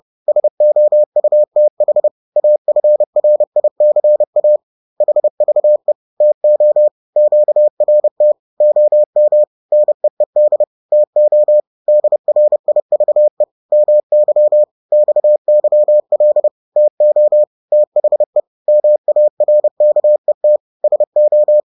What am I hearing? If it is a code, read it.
Morse code, heard as a keyed tone, about 30 wpm, 600 Hz. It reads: SOUTH AFRICA HVE TO QRT OM NEED TO DRIVE MY XYL TO THE MARKET SO